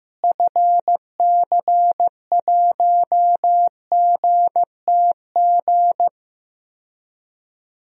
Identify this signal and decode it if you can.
Morse code, heard as a keyed tone, about 15 wpm, 700 Hz. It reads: FC1GTG
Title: DXCC-100 Call Signs - 6 Characters - Repeated 3x - 15wpm